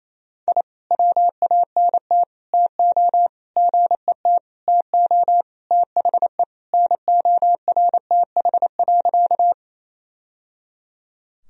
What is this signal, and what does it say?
Morse code, heard as a keyed tone, about 28 wpm, 700 Hz. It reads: I WANT TO GET TO THE NORTH.
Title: Sentences from Top 1000 Words - Review Part 1 - 28wpm